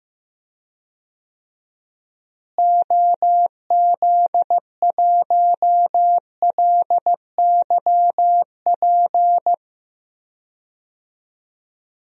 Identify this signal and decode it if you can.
Morse code, heard as a keyed tone, about 15 wpm, 700 Hz. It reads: OZ1LYP
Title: DXCC-100 Call Signs - 6 Characters - Repeated 3x - 15wpm